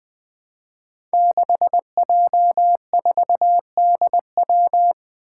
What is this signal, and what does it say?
Morse code, heard as a keyed tone, about 20 wpm, 700 Hz. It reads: 6J4DW
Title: Five Challenging Letter-Number Combinations 20wpm